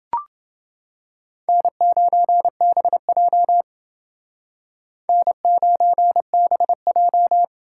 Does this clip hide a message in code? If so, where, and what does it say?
Yes: Morse code from 1.5 to 7.4 s, heard as a keyed tone, about 29 wpm, 700 Hz. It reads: N9BJ N9BJ